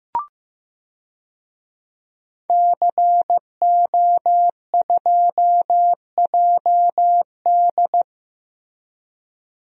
Morse code, heard as a keyed tone, about 15 wpm, 700 Hz. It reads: CO2JD
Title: DXCC-100 Call Signs - 5 Characters - Repeated 3x - 15wpm